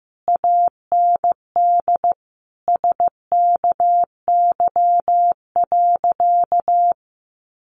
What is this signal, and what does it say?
Morse code, heard as a keyed tone, about 15 wpm, 700 Hz. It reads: AND SKY.
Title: Sentences from Top 1200 Words 15wpm